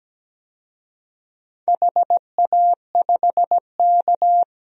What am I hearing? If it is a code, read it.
Morse code, heard as a keyed tone, about 17 wpm, 700 Hz. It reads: HA5K
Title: DXCC-100 Call Signs - 4 Characters - Repeated 2x - 17wpm